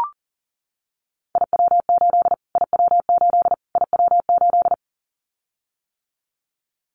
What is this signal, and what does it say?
Morse code, heard as a keyed tone, about 40 wpm, 700 Hz. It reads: IW8 IW8 IW8
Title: Callsign Prefixes for Top 25 DXCC - Repeated 3x - 40wpm